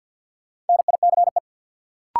Morse code, heard as a keyed tone, about 52 wpm, 700 Hz. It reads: NICE
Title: Sentences from Top 1100 Words - New Word Drill Alt - 50wpm